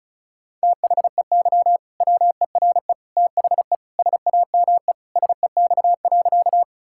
Morse code, heard as a keyed tone, about 34 wpm, 700 Hz. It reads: THEY WERE THE SAME SEX.
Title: Sentences from Top 1600 Words 35wpm